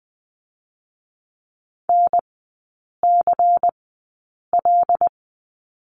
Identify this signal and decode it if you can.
Morse code, heard as a keyed tone, about 19 wpm, spaced as for 8 wpm, 700 Hz. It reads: NCL
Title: 100 Lesson 17e - Letter u - Three Characters 20wpm